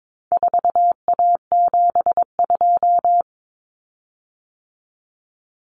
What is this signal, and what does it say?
Morse code, heard as a keyed tone, about 22 wpm, 700 Hz. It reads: 4A72
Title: Common Cut Numbers - 4-Digit 22wpm